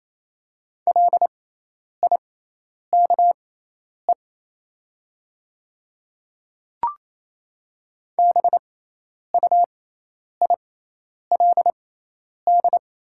Morse code, heard as a keyed tone, about 28 wpm, spaced as for 10 wpm, 700 Hz. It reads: LIKE BUILD
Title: 6x Character Spacing - Top 300 Words - 28wpm